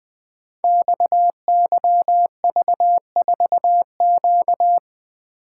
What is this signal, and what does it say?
Morse code, heard as a keyed tone, about 20 wpm, 700 Hz. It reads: XYV4Q
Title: Code Groups - 5 Letter:Numbers - 20wpm